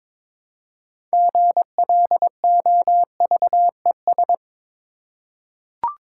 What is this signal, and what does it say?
Morse code, heard as a keyed tone, about 21 wpm, 700 Hz. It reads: GLOVES
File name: Difficult Words with Hints - 22wpm